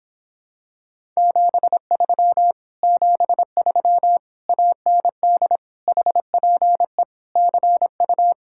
Morse code, heard as a keyed tone, about 26 wpm, 700 Hz. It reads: 73 73 AND HPE CU